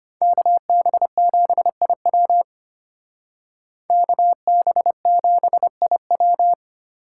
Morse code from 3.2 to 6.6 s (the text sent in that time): KB7IW